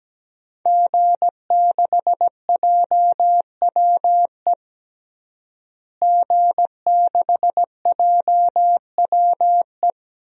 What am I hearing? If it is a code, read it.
Morse code, heard as a keyed tone, about 17 wpm, 700 Hz. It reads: G6JWE G6JWE
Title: DXCC-100 Call Signs - 5 Characters - Repeated 3x - 17wpm